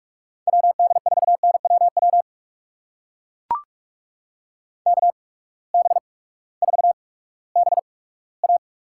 Morse code, heard as a keyed tone, about 44 wpm, spaced as for 20 wpm, 700 Hz. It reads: WD4NWW KB4BA
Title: USA Call Signs - 6 Characters - 8x Character Spacing - 45wpm